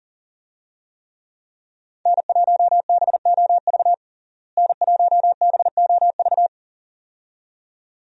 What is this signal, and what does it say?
Morse code, heard as a keyed tone, about 37 wpm, 700 Hz. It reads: N1BOV N1BOV
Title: USA Call Signs - 5 Characters - Repeated 2x - 40wpm